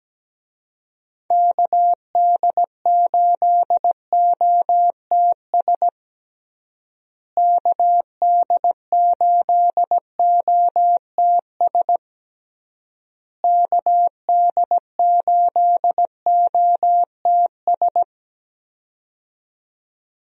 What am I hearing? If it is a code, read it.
Morse code, heard as a keyed tone, about 17 wpm, 700 Hz. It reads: KD8OTS KD8OTS KD8OTS
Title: USA Call Signs - 6 Characters - Repeated 3x - 17wpm